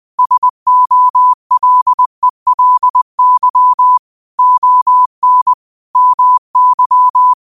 Morse code, heard as a keyed tone, about 20 wpm, 1000 Hz. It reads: SOLELY ON MY
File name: qotdfiles / 20wpm